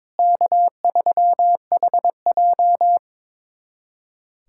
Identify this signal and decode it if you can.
Morse code, heard as a keyed tone, about 22 wpm, 700 Hz. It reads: K3HJ
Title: USA Callsign DE Callsign - 4 Characters - 22wpm